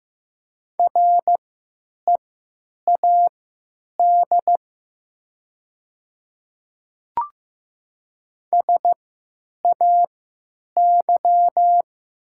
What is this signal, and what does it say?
Morse code, heard as a keyed tone, about 15 wpm, spaced as for 9 wpm, 700 Hz. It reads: READ SAY
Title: 3x Character Spacing - Top 200 Words - 15wpm